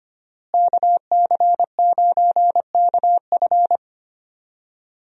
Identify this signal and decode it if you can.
Morse code, heard as a keyed tone, about 25 wpm, 700 Hz. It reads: KC9KF
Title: USA Call Signs - 5 Characters - Repeated 2x - 25wpm